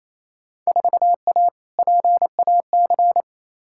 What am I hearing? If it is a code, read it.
Morse code, heard as a keyed tone, about 28 wpm, 700 Hz. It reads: 4A PAC
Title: Field Day Exchanges Encore - 28wpm